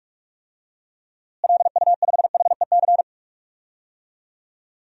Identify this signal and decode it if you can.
Morse code, heard as a keyed tone, about 47 wpm, 700 Hz. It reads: LU5HEC
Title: DXCC-100 Call Signs - 6 Characters - Repeated 2x - 45wpm